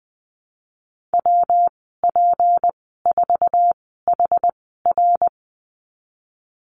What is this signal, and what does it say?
Morse code, heard as a keyed tone, about 20 wpm, spaced as for 15 wpm, 700 Hz. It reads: WP4HR
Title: 167 Lesson 27g - Number 7 - Call Signs 20wpm